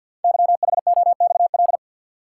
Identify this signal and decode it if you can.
Morse code, heard as a keyed tone, about 48 wpm, 700 Hz. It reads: YHOXL
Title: Code Groups - 5 Letters - 50wpm